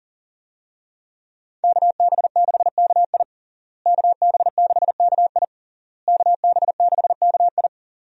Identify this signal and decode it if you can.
Morse code, heard as a keyed tone, about 40 wpm, 700 Hz. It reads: KB6KI KB6KI KB6KI
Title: USA Call Signs - 5 Characters - Repeated 3x - 40wpm